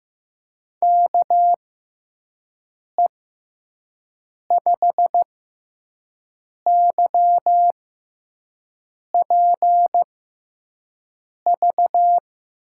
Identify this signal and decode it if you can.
Morse code, heard as a keyed tone, about 15 wpm, spaced as for 5 wpm, 700 Hz. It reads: KE5YPV
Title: USA Call Signs - 6 Characters - 6x Character Spacing - 15wpm